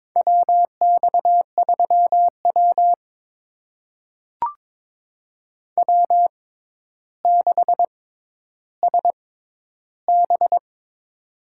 Morse code, heard as a keyed tone, about 22 wpm, spaced as for 11 wpm, 700 Hz. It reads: WX3W W6SB
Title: USA Call Signs - 4 Characters - 6x Character Spacing - 22wpm